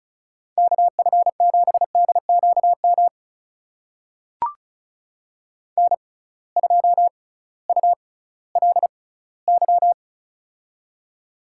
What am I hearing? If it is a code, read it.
Morse code, heard as a keyed tone, about 36 wpm, spaced as for 19 wpm, 700 Hz. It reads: KF7DQM N2ULY
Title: USA Call Signs - 6x Character Spacing - 35wpm